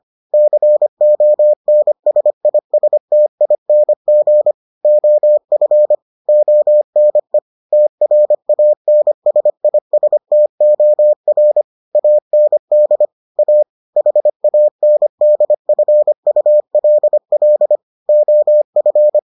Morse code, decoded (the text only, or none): CONSISTING OF ONE TRANSISTOR AND A HANDFULL OF